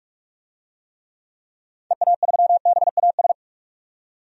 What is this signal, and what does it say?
Morse code, heard as a keyed tone, about 42 wpm, 700 Hz. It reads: EA3BAS
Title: DXCC-100 Call Signs - 6 Characters - 45wpm